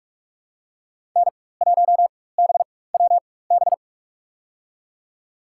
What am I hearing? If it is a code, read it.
Morse code, heard as a keyed tone, about 48 wpm, spaced as for 21 wpm, 700 Hz. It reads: N1BWB
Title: USA Call Signs - 5 Characters - 4x Character Spacing - 45wpm